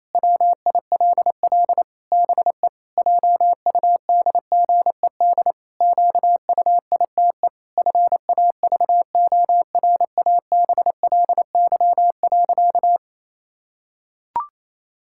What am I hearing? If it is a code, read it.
Morse code, heard as a keyed tone, about 28 wpm, 700 Hz. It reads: WILL BE JUDGED QUITE FAVORABLY.
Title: Pangram Challenge v2 - 28wpm